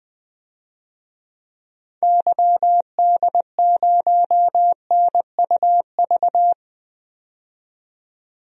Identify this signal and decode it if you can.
Morse code, heard as a keyed tone, about 20 wpm, 700 Hz. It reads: YD0NUV
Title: DXCC-100 Call Signs - 6 Characters - 20wpm